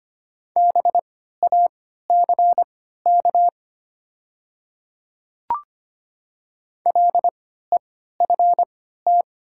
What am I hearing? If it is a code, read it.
Morse code, heard as a keyed tone, about 25 wpm, spaced as for 14 wpm, 700 Hz. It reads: BACK LEFT
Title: 3x Character Spacing - Top 300 Words - 25wpm